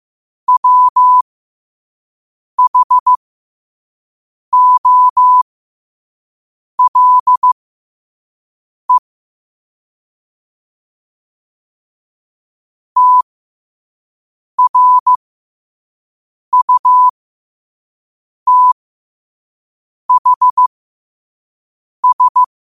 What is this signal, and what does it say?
Morse code, heard as a keyed tone, about 15 wpm, spaced as for 5 wpm, 1000 Hz. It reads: WHOLE TRUTHS
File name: qotdfiles / 5wpm